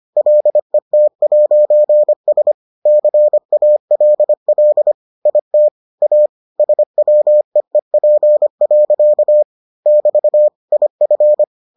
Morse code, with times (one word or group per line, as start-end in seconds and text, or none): 0.2-2.5 LET'S
2.9-4.9 CALL
5.3-5.7 IT
6.0-6.3 A
6.6-9.4 SWEEP.
9.9-10.5 <BT>
10.7-11.4 IF